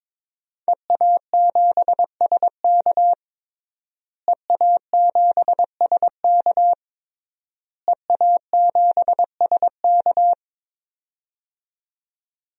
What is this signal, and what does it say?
Morse code, heard as a keyed tone, about 22 wpm, 700 Hz. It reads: EA7SK EA7SK EA7SK
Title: DXCC-100 Call Signs - 5 Characters - Repeated 3x - 22wpm